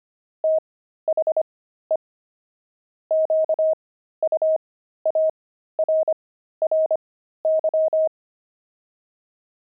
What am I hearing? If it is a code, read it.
Morse code, heard as a keyed tone, about 25 wpm, spaced as for 13 wpm, 630 Hz. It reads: THE QUARRY
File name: pangram-fs-13-cs-25-steaks